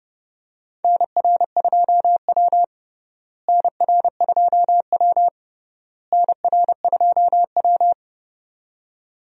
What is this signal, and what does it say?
Morse code, heard as a keyed tone, about 30 wpm, 700 Hz. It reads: NR2W NR2W NR2W
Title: USA Call Signs - 4 Characters - Repeated 3x - 30wpm